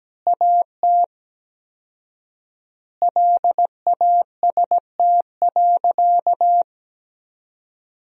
Morse code, heard as a keyed tone, about 17 wpm, 700 Hz. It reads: AT LAST.
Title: 4x Word-Spacing - Sentences from Top 300 Words - 17wpm